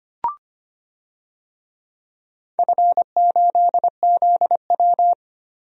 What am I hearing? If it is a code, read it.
Morse code, heard as a keyed tone, about 25 wpm, 700 Hz. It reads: F8ZW
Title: DXCC-100 Call Signs - 4 Characters - 25wpm